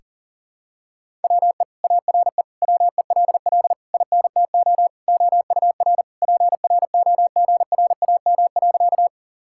Morse code, heard as a keyed tone, about 40 wpm, 700 Hz. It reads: WE ARE WELL INTO OUR PROGRAM.
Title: Sentences from Top 1000 Words - Review Part 5 - 40wpm